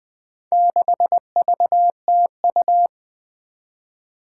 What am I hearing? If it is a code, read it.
Morse code, heard as a keyed tone, about 20 wpm, 700 Hz. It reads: 6VTU
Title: Four Challenging Letter-Number Combinations 20wpm